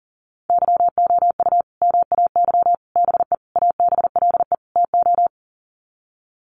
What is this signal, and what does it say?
Morse code, heard as a keyed tone, about 40 wpm, 700 Hz. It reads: YOU MAY BE ABLE TO
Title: Sets of 5 Words Encore 40wpm